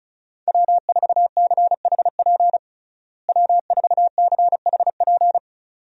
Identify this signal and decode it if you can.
Morse code, heard as a keyed tone, about 36 wpm, 700 Hz. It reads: W4CHP W4CHP
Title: USA Call Signs - 5 Characters - Repeated 2x - 35wpm